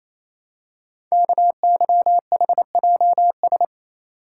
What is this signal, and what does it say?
Morse code, heard as a keyed tone, about 27 wpm, 700 Hz. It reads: KYHJS
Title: Code Groups - 5 Letters - 28wpm